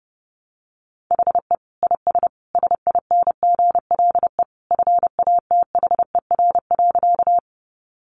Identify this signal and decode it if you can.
Morse code, heard as a keyed tone, about 30 wpm, 700 Hz. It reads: HE IS SINGLE FATHER.
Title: Sentences from Top 700 Words 30wpm